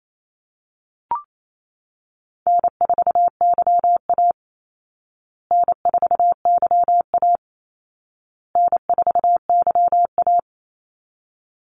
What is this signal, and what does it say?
Morse code, heard as a keyed tone, about 28 wpm, 700 Hz. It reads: N4YA N4YA N4YA
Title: Call Signs Repeated 2x - 28wpm